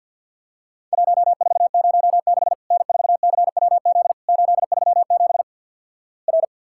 Morse code, heard as a keyed tone, about 50 wpm, 700 Hz. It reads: 1406 N4CWZ 837 R
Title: Worked All Europe DX Contest - QTC Exchange - 50wpm